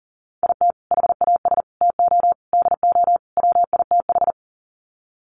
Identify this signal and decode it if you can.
Morse code, heard as a keyed tone, about 40 wpm, 700 Hz. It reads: IT HAS TO DO WITH
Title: Sets of 5 Words 40wpm